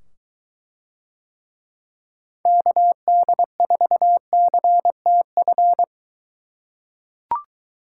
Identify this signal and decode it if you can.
Morse code, heard as a keyed tone, about 22 wpm, 700 Hz. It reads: KD4CTF